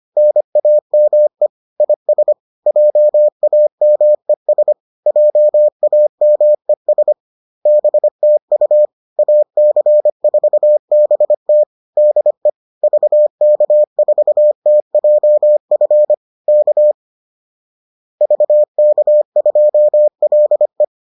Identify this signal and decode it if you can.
Morse code, heard as a keyed tone, about 25 wpm, 600 Hz. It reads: NAME IS JAMES JAMES BTU AC4BT DE VK4TJF K VK2LE